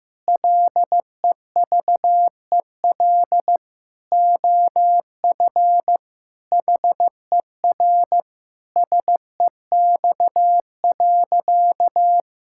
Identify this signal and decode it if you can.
Morse code, heard as a keyed tone, about 15 wpm, 700 Hz. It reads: LEVEL OF HER SEX.
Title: Sentences from Top 1600 Words 15wpm